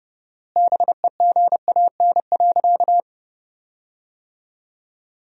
Morse code, heard as a keyed tone, about 30 wpm, 700 Hz. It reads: BEGAN.
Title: 4x Word-Spacing - Sentences from Top 300 Words - 30wpm